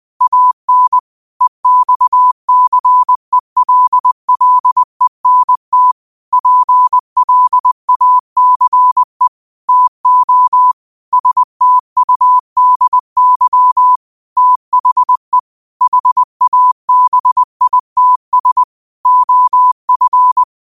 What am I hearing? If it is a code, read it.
Morse code, heard as a keyed tone, about 20 wpm, 1000 Hz. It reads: AN EXCELLENT PLACE TO STUDY THE HABITS OF